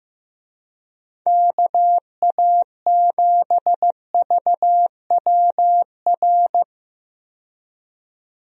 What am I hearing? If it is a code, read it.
Morse code, heard as a keyed tone, about 15 wpm, 700 Hz. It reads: KA7VWR
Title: USA Call Signs - 6 Characters - 15wpm